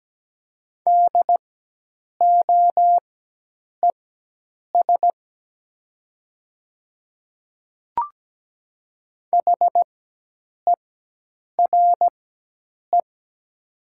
Morse code, heard as a keyed tone, about 17 wpm, spaced as for 8 wpm, 700 Hz. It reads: DOES HERE
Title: 4x Character Spacing - Top 500 Words - 17wpm